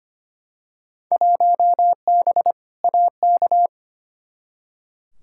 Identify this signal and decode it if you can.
Morse code, heard as a keyed tone, about 25 wpm, 700 Hz. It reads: 1B AK
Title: Field Day Exchanges Encore - 25wpm